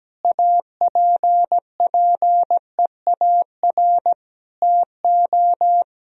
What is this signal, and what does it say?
Morse code, heard as a keyed tone, about 17 wpm, 700 Hz. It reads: APPEAR TO
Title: Sentences from Top 1000 Words - Review Part 2 - 17wpm